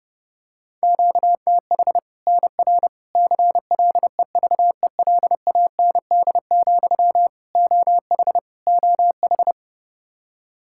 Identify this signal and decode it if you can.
Morse code, heard as a keyed tone, about 30 wpm, 700 Hz. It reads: QTH NR CLEVELAND, OH OH
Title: K1USN SST QSO Practice 30wpm